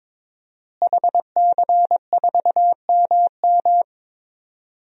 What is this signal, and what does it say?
Morse code, heard as a keyed tone, about 21 wpm, 700 Hz. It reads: HC4MM
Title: DXCC-100 Call Signs - 5 Characters - Repeated 2x - 22wpm